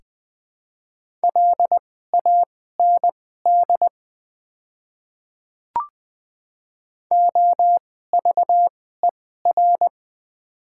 Morse code, heard as a keyed tone, about 20 wpm, spaced as for 14 wpm, 700 Hz. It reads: LAND OVER
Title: 2x Character Spacing - Top 300 Words - 20wpm